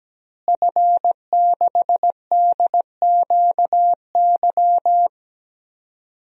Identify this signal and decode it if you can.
Morse code, heard as a keyed tone, about 17 wpm, 700 Hz. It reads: F6DQY
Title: DXCC-100 Call Signs - 5 Characters - Repeated 3x - 17wpm